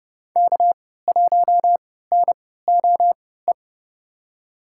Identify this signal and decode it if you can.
Morse code, heard as a keyed tone, about 29 wpm, spaced as for 17 wpm, 700 Hz. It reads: K1NOE
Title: USA Call Signs - 5 Characters - 3x Character Spacing - 30wpm